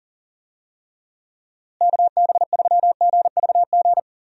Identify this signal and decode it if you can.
Morse code, heard as a keyed tone, about 40 wpm, 700 Hz. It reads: KB3GVG
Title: USA Call Signs - 6 Characters - Repeated 3x - 40wpm